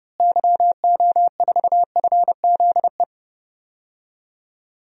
Morse code, heard as a keyed tone, about 30 wpm, 700 Hz. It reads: YO4FZE
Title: DXCC-100 Call Signs - 6 Characters - Repeated 2x - 30wpm